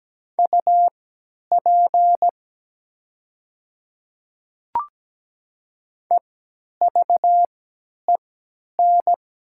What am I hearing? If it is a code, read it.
Morse code, heard as a keyed tone, about 17 wpm, spaced as for 10 wpm, 700 Hz. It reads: UP EVEN
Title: 3x Character Spacing - Top 500 Words - 17wpm